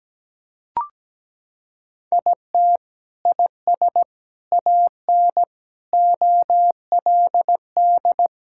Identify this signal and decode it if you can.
Morse code, heard as a keyed tone, about 17 wpm, 700 Hz. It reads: IT IS AN OLD
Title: Sentences from Top 1600 Words 17wpm